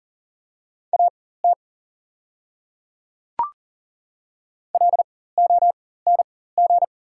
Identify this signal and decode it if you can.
Morse code, heard as a keyed tone, about 40 wpm, spaced as for 19 wpm, 690 Hz. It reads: AT LONG
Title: 4x Character Spacing - Top 100 Words - 40wpm